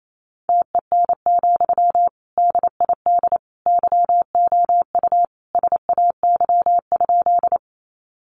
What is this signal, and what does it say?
Morse code, heard as a keyed tone, about 28 wpm, 700 Hz. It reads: TEN, DID YOU SAY?
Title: Sentences from Top 400 Words 28wpm